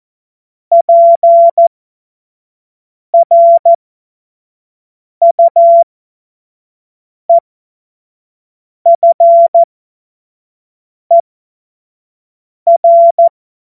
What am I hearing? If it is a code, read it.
Morse code, heard as a keyed tone, about 14 wpm, spaced as for 5 wpm, 680 Hz. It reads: PRUEFER
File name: Morsekurs / Witze / Witz 17. Mai 2022 25 F70